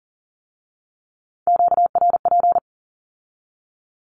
Morse code, heard as a keyed tone, about 42 wpm, 700 Hz. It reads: QRP
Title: Q-Codes with Spoken Meaning - 40wpm